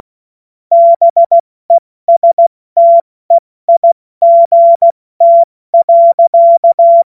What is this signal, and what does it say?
Morse code, heard as a keyed tone, about 16 wpm, 680 Hz. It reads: BESTEIGT.
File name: Morsekurs / Witze / Witz 13. April 2021, 70 F80